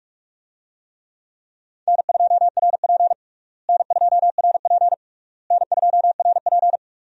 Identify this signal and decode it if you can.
Morse code, heard as a keyed tone, about 46 wpm, 700 Hz. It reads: N2RP N2RP N2RP
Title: USA Call Signs - 4 Characters - Repeated 3x - 45wpm